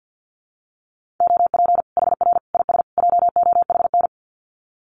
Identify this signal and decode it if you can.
Morse code, heard as a keyed tone, about 48 wpm, 700 Hz. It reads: OP HR IS JOHN
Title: Common QSO Phrases 50wpm